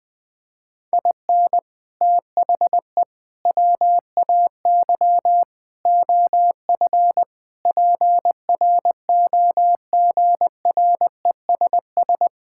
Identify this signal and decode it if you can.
Morse code, heard as a keyed tone, about 21 wpm, 700 Hz. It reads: IN THE WAY OF PROGRESS
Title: Sentences with 4-Word Phrases - 20wpm